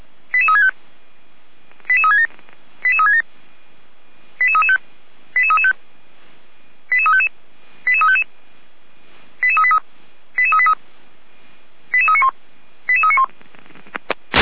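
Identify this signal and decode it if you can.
Five-tone selective call (ZVEI), heard as quick runs of five tones, tones of about 70 ms. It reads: 88365 88367 88367 88335 88335 88355 88355 88383 88383 88391 88391
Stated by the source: ZVEI five-tone call received by SDR.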